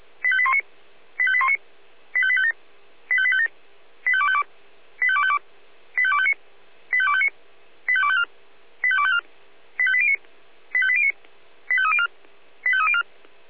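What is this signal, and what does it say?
Five-tone selective call (ZVEI), heard as quick runs of five tones, tones of about 70 ms. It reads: 85819 85819 85857 85857 85252 85252 85259 85259 85254 85254 85809 85809 85304 85304
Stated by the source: ZVEI five-tone selective call 85832 received on SDR.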